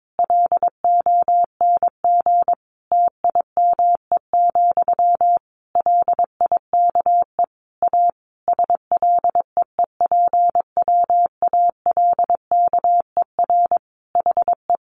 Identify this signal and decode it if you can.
Morse code, heard as a keyed tone, about 22 wpm, 700 Hz. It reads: LONG TIME, LIKE A SLEEPWALKER HE
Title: A Sound Of Thunder 22wpm